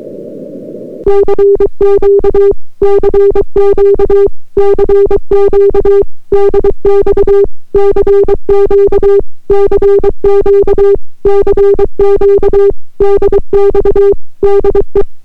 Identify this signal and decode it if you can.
Morse code, heard as a keyed tone, about 23 wpm, 400 Hz. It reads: CQ CQ CQ DX CQ CQ CQ DX DE